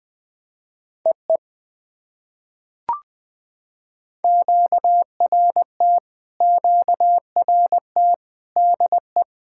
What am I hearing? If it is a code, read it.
Morse code, heard as a keyed tone, about 20 wpm, 700 Hz. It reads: EE QRT QRT DE
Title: POTA QSO Practice - 20wpm